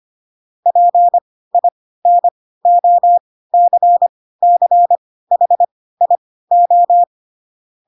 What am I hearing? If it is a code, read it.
Morse code, heard as a keyed tone, about 25 wpm, spaced as for 16 wpm, 700 Hz. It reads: PINOCCHIO